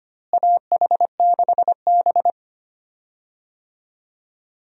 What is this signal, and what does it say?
Morse code, heard as a keyed tone, about 25 wpm, 700 Hz. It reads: AH6B
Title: USA Call Signs - 4 Characters - Repeated 2x - 25wpm